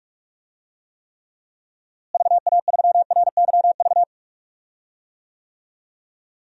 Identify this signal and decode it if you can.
Morse code, heard as a keyed tone, about 43 wpm, 700 Hz. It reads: VA3RYV
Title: DXCC-100 Call Signs - 6 Characters - Repeated 2x - 45wpm